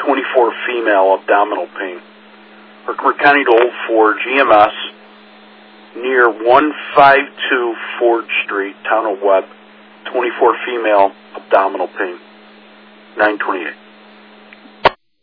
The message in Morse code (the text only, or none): none